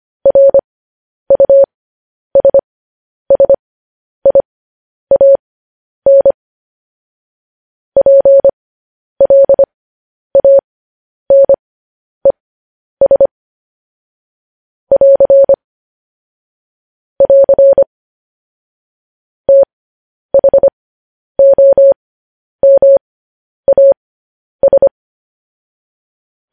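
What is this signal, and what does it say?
Morse code, heard as a keyed tone, about 25 wpm, spaced as for 10 wpm, 550 Hz. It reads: RUSSIAN PLANES <AR> <AR> THOMAS
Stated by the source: Morse code news headlines sent as a reading transcription, preceded by vvv.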